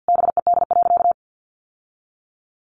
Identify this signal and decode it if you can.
Morse code, heard as a keyed tone, about 50 wpm, 700 Hz. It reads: BED.
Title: Sentences from Top 1200 Words 50wpm